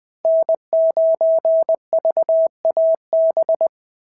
Morse code, heard as a keyed tone, about 21 wpm, 690 Hz. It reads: N9VAB
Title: SOTA QSO Practice 20wpm